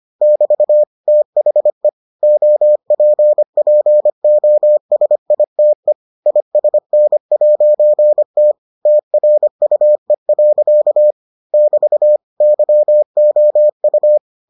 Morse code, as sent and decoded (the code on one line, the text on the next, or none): -...- / - .... / --- .--. .--. --- ... .. - . / .. ... -. .----. - / - .-. ..- . .-.-.- / -...- / -.-- --- ..-
<BT> THE OPPOSITE ISN'T TRUE. <BT> YOU